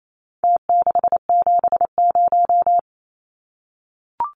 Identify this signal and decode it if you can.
Morse code, heard as a keyed tone, about 28 wpm, 700 Hz. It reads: T670
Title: Numbers - Contest Style - 4-Digit 28wpm